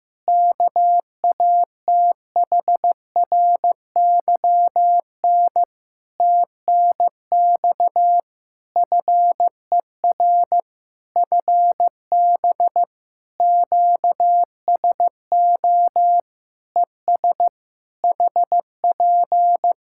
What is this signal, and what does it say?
Morse code, heard as a keyed tone, about 15 wpm, 700 Hz. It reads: KATHRYN TNX FER FB QSO ES HP
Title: Practice QSO - Exchange 4 - Ending - 15wpm